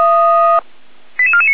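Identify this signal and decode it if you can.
Five-tone selective call (ZVEI), heard as quick runs of five tones, tones of about 70 ms. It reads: 88449
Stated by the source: ZVEI five-tone selective call received by SDR, fire brigade Feuerwehr Orlamünde.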